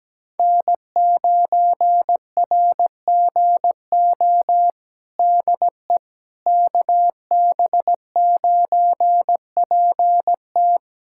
Morse code, heard as a keyed tone, about 17 wpm, 700 Hz. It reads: N9RGO DE KB9PT
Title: USA Callsign DE Callsign - 5 Characters - 17wpm